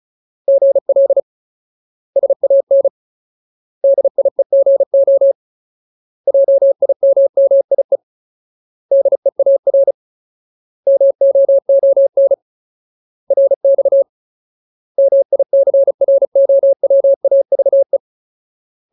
Morse code, heard as a keyed tone, about 34 wpm, 550 Hz. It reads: GL SAN DIEGO JIMMIE DEAR MOON RX MICROWAVE
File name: Ultimate Word List / 35WPM Ultimate